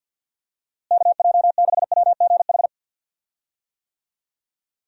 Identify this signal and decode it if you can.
Morse code, heard as a keyed tone, about 50 wpm, 700 Hz. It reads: KJ6WGH
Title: USA Call Signs - 6 Characters - Repeated 2x - 50wpm